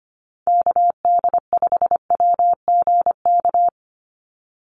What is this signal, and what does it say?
Morse code, heard as a keyed tone, about 25 wpm, 700 Hz. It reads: KD5WGK
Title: USA Call Signs 25wpm